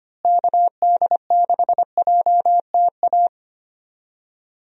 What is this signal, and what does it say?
Morse code, heard as a keyed tone, about 25 wpm, 700 Hz. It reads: KD6JTA
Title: USA Call Signs - 6 Characters - Repeated 3x - 25wpm